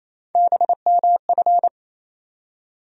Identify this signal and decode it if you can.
Morse code, heard as a keyed tone, about 28 wpm, 700 Hz. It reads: BMF
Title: USA Call Sign Postfixes - 3 Letters 28wpm